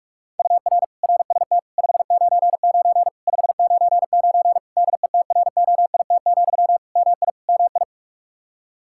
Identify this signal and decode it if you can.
Morse code, heard as a keyed tone, about 41 wpm, 700 Hz. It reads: UR RST 599 599 DETROIT, MI MI
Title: Practice QSO - Exchange 1 - Beginning - 45wpm